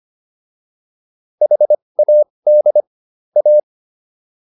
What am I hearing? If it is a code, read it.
Morse code, heard as a keyed tone, about 25 wpm, spaced as for 20 wpm, 600 Hz. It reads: HAD A